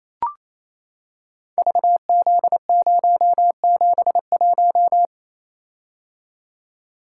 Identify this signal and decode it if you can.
Morse code, heard as a keyed tone, about 28 wpm, 700 Hz. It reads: VZ071
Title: Code Groups - 5 Letter:Numbers - 28wpm